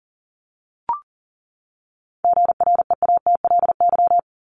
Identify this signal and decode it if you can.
Morse code, heard as a keyed tone, about 39 wpm, 700 Hz. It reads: GREATLY
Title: Top 1251-1500 Words 40wpm